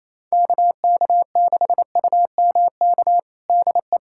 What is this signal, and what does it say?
Morse code, heard as a keyed tone, about 28 wpm, 700 Hz. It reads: KK6UMK DE
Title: USA Callsign DE Callsign 28wpm